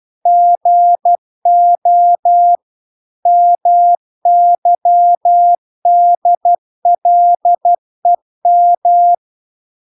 Morse code, heard as a keyed tone, about 12 wpm, 700 Hz. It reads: GO MYDLEM